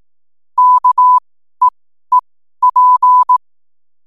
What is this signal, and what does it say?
Morse code, heard as a keyed tone, about 18 wpm, spaced as for 12 wpm, 1000 Hz. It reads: KEEP
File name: n7net / 12wpm